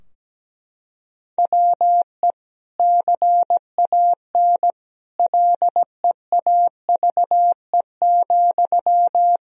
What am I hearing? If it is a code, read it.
Morse code, heard as a keyed tone, about 17 wpm, 700 Hz. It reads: WE CAN LEAVE,
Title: Sentences from Top 1000 Words - Review Part 4 - 17wpm